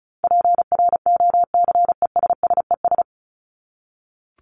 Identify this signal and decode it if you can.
Morse code, heard as a keyed tone, about 34 wpm, 700 Hz. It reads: PROCESSES
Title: ES:S Suffix - Top 100 Words with Base Words 35wpm